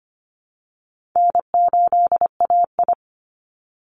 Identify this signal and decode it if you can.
Morse code, heard as a keyed tone, about 25 wpm, 700 Hz. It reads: N8AI
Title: Call Signs Repeated 3x - 25wpm